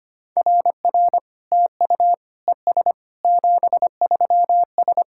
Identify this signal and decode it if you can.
Morse code, heard as a keyed tone, about 25 wpm, 690 Hz. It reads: RR TU ES 73S
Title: SOTA QSO Practice 25wpm